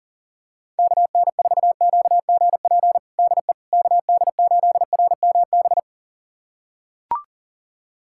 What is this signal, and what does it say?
Morse code, heard as a keyed tone, about 40 wpm, 700 Hz. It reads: KN4QGP DE KD8RMB
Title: USA Callsign DE Callsign 40wpm